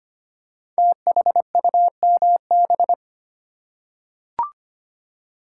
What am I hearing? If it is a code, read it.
Morse code, heard as a keyed tone, about 25 wpm, 700 Hz. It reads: THUMB
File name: 5 Letter Words - First 3 Letters Given - 25wpm